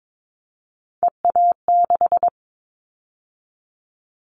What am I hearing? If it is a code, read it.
Morse code, heard as a keyed tone, about 22 wpm, 700 Hz. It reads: EA6
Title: Callsign Prefixes for Top 50 DXCC - 22wpm